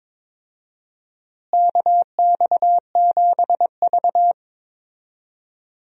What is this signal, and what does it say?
Morse code, heard as a keyed tone, about 22 wpm, 700 Hz. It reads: KX7V
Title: USA Call Signs - 4 Characters - Repeated 3x - 22wpm